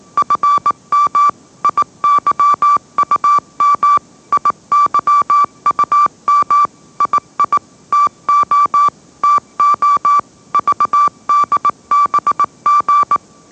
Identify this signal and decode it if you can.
Morse code, heard as a keyed tone, about 18 wpm, 1160 Hz. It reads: FM IYUM IYUM II TO TO VDBG
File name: mrx / mrx081 20wpm